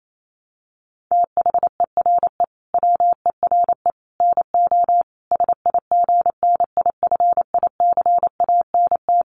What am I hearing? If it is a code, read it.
Morse code, heard as a keyed tone, about 28 wpm, 700 Hz. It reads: THERE WERE NO SIGNIFICANT
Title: Sets of 5 Words Encore2 28wpm